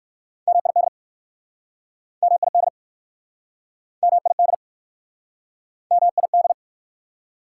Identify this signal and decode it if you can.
Morse code, heard as a keyed tone, about 60 wpm, 690 Hz. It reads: MID MID MID MID